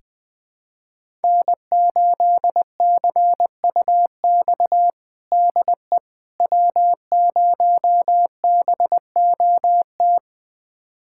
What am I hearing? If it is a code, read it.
Morse code, heard as a keyed tone, about 20 wpm, 700 Hz. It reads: N8CUX DE W0BOT
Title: USA Callsign DE Callsign - 5 Characters - 20wpm